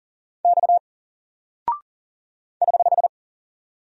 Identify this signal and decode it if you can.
Morse code, heard as a keyed tone, about 39 wpm, 700 Hz. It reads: X <HH>
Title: Single Character (Letter:Number:Punctuation:Prosign) 40wpm